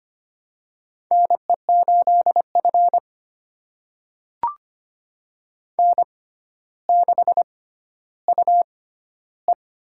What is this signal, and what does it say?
Morse code, heard as a keyed tone, about 25 wpm, spaced as for 13 wpm, 700 Hz. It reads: NE8F N6UE
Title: USA Call Signs - 4 Characters - 6x Character Spacing - 25wpm